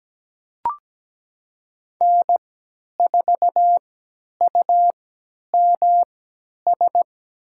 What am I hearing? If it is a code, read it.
Morse code, heard as a keyed tone, about 17 wpm, spaced as for 10 wpm, 700 Hz. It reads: N4UMS
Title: USA Call Signs - 5 Characters - 3x Character Spacing - 17wpm